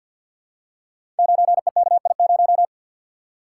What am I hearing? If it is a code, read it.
Morse code, heard as a keyed tone, about 47 wpm, 700 Hz. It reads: 9EKI0
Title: Code Groups - 5 Letter:Numbers - 50wpm